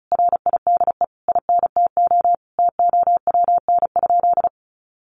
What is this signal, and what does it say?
Morse code, heard as a keyed tone, about 34 wpm, 700 Hz. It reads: RIDE INTO TOWN?
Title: Sentences from Top 1400 Words 35wpm